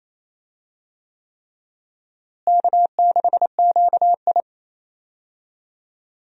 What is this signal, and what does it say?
Morse code, heard as a keyed tone, about 28 wpm, 700 Hz. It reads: K6QI
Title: USA Call Signs - 4 Characters - Repeated 3x - 28wpm